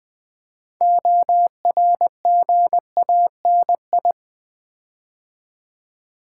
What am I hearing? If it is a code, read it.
Morse code, heard as a keyed tone, about 20 wpm, 700 Hz. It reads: ORGANI
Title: Build A Word - 6 Letters At A Time - 12-Letter Words - 20wpm